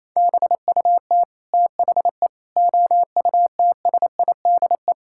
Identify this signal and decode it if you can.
Morse code, heard as a keyed tone, about 27 wpm, 700 Hz. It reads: BUT THE OUTSIDE
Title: Sentences from Top 1000 Words - Review Part 6 - 28wpm